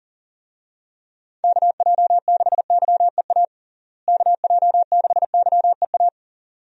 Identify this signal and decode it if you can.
Morse code, heard as a keyed tone, about 40 wpm, 700 Hz. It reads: KJ6YEA KJ6YEA
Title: USA Call Signs - 6 Characters - Repeated 2x - 40wpm